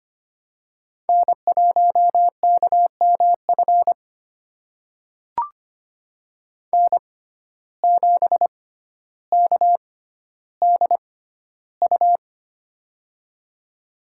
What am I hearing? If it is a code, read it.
Morse code, heard as a keyed tone, about 25 wpm, spaced as for 13 wpm, 700 Hz. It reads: N1KMF N7KDU
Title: USA Call Signs - 5 Characters - 6x Character Spacing - 25wpm